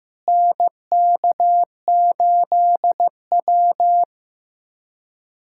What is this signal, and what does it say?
Morse code, heard as a keyed tone, about 15 wpm, 700 Hz. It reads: NK8W
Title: USA Callsign DE Callsign - 4 Characters - 15wpm